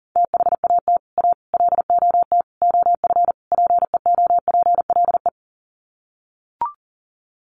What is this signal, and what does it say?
Morse code, heard as a keyed tone, about 40 wpm, 700 Hz. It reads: THAT A LOT OF PEOPLE
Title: Sets of 5 Words 40wpm